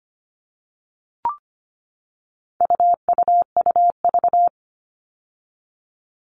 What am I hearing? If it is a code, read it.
Morse code, heard as a keyed tone, about 25 wpm, 700 Hz. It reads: UUUV
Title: All Cut Numbers - 4-Digit 25wpm